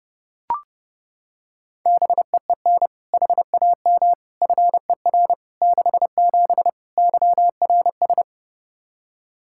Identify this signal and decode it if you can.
Morse code, heard as a keyed tone, about 32 wpm, 700 Hz. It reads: BEEN HAM FER 67 YRS
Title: Practice QSO - Exchange 3 - More Info - 30wpm